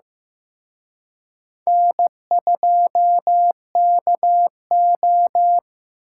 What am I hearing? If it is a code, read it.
Morse code, heard as a keyed tone, about 18 wpm, 700 Hz. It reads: N2KO
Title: Speed Racing - USA 1x2 Call Signs 15wpm